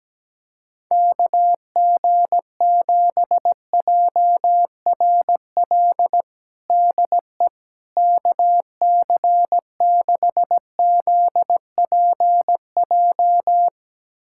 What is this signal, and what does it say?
Morse code, heard as a keyed tone, about 17 wpm, 700 Hz. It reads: KG7JRL DE KC6ZPJ
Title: USA Callsign DE Callsign - 6 Characters - 17wpm